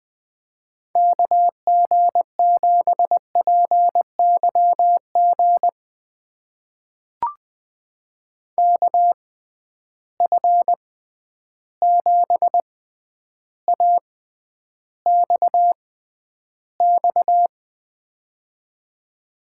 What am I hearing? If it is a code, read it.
Morse code, heard as a keyed tone, about 20 wpm, spaced as for 10 wpm, 700 Hz. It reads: KG7PYG KF7AXX
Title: USA Call Signs - 6 Characters - 6x Character Spacing - 20wpm